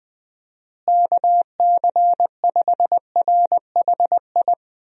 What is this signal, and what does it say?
Morse code, heard as a keyed tone, about 20 wpm, 700 Hz. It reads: KC5RHI
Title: USA Call Signs - 6 Characters - Repeated 3x - 20wpm